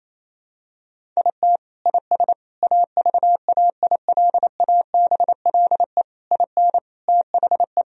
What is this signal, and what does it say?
Morse code, heard as a keyed tone, about 28 wpm, 700 Hz. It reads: IT IS AVAILABLE IN THE
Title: Sentences from Top 1000 Words - Review Part 3 - 28wpm